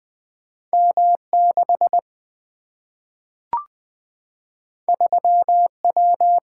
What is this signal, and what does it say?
Morse code, heard as a keyed tone, about 20 wpm, 700 Hz. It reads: M 6 3 W
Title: Single-Letter Single-Number 20wpm